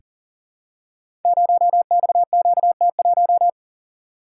Morse code, heard as a keyed tone, about 39 wpm, 700 Hz. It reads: 0XQT1
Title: Code Groups - 5 Letter:Numbers - 40wpm